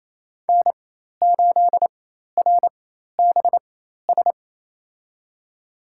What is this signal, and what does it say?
Morse code, heard as a keyed tone, about 28 wpm, spaced as for 13 wpm, 700 Hz. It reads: N8RBS
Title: USA Call Signs - 4x Character Spacing - 28wpm